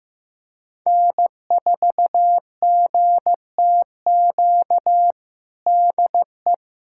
Morse code, heard as a keyed tone, about 15 wpm, 700 Hz. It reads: N4GTQ DE
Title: USA Callsign DE Callsign - 5 Characters - 15wpm